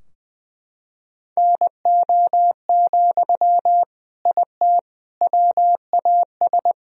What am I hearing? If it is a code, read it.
Morse code, heard as a keyed tone, about 21 wpm, 700 Hz. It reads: NO, IT WAS
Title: Sentences from Top 1000 Words - Review Part 4 - 20wpm